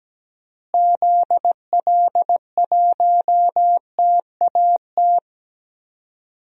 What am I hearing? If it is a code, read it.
Morse code, heard as a keyed tone, about 17 wpm, 700 Hz. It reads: ZL1TAT
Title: DXCC-100 Call Signs - 6 Characters - Repeated 3x - 17wpm